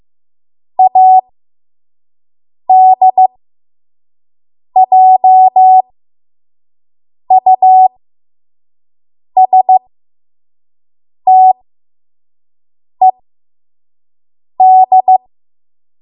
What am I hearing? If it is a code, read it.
Morse code, heard as a keyed tone, about 15 wpm, spaced as for 5 wpm, 750 Hz. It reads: ADJUSTED
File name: Morse / 05WPM